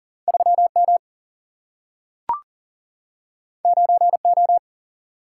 Morse code, heard as a keyed tone, about 40 wpm, 700 Hz. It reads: 3M 9O
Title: Single-Letter Single-Number 40wpm